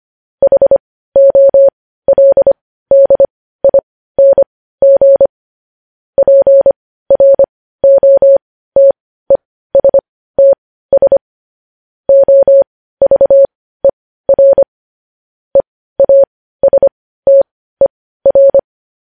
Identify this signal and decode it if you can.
Morse code, heard as a keyed tone, about 25 wpm, spaced as for 15 wpm, 550 Hz. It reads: HOLDING PROTESTS OVER EASTER
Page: News Headlines in Morse Code at 15 WPM